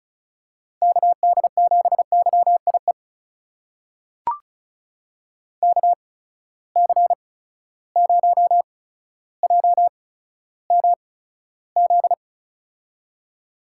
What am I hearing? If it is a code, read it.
Morse code, heard as a keyed tone, about 36 wpm, spaced as for 15 wpm, 700 Hz. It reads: KD7YIE KC0JMZ
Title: USA Call Signs - 6 Characters - 8x Character Spacing - 35wpm